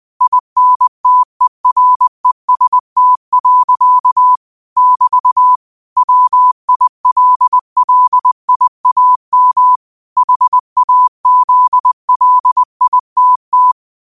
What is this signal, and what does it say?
Morse code, heard as a keyed tone, about 20 wpm, 1000 Hz. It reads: INTEREST. <BT> WILLIAM HAZLITT
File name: qotdfiles / 20wpm